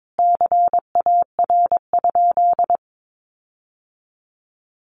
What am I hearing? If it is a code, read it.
Morse code, heard as a keyed tone, about 22 wpm, 700 Hz. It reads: CAR?
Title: Sentences from Top 1500 Words 22wpm